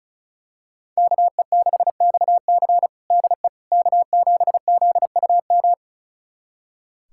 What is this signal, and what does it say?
Morse code, heard as a keyed tone, about 34 wpm, 700 Hz. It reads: KE6XC DE K7ZUM
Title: USA Callsign DE Callsign - 5 Characters - 35wpm